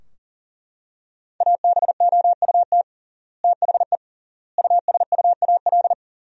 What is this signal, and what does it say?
Morse code, heard as a keyed tone, about 40 wpm, 700 Hz. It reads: ABOUT THE USUAL
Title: 3x Word-Spacing - Sentences from Top 400 Words - 40wpm